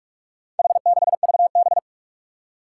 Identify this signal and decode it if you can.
Morse code, heard as a keyed tone, about 46 wpm, 700 Hz. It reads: H6VB
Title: Four Challenging Letter-Number Combinations 45wpm